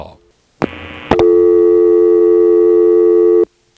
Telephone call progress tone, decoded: dial tone